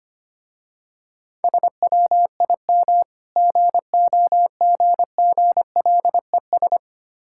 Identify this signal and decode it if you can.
Morse code, heard as a keyed tone, about 25 wpm, 700 Hz. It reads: SWIM GOGGLES
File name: practice / 25wpm / Difficult Words with Hints - 25wpm